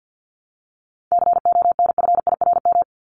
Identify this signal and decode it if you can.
Morse code, heard as a keyed tone, about 48 wpm, 700 Hz. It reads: CONFIRM
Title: Top 1751-2000 Words 50wpm